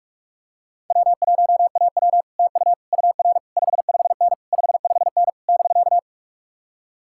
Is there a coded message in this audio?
Yes: Morse code from 0.9 to 6.0 s, heard as a keyed tone, about 50 wpm, 700 Hz. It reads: W1AW TU UR 55N 55N <BK>